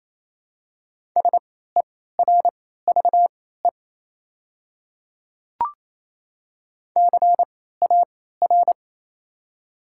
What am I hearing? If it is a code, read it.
Morse code, heard as a keyed tone, about 27 wpm, spaced as for 16 wpm, 700 Hz. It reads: SERVE CAR
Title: 3x Character Spacing - Top 500 Words - 28wpm